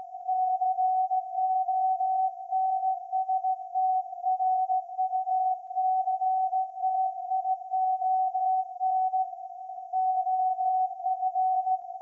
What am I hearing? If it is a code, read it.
Morse code, heard as a keyed tone, about 16 wpm, 750 Hz. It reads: CONSTRUCTION OF
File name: iz7slz / VLF / SAQ 30-06-2024 0900